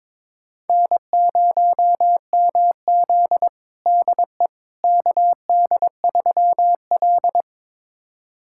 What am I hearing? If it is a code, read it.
Morse code, heard as a keyed tone, about 22 wpm, 700 Hz. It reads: N0MZ DE KD3L
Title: USA Callsign DE Callsign - 4 Characters - 22wpm